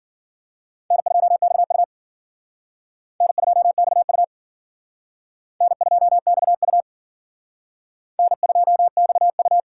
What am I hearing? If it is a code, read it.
Morse code, heard as a keyed tone, about 52 wpm, 700 Hz. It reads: N2XU N2XU N2XU N2XU